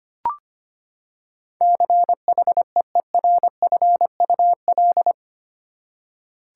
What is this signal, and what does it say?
Morse code, heard as a keyed tone, about 25 wpm, 700 Hz. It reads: CHEERFUL
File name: Eight Letter Words - 25wpm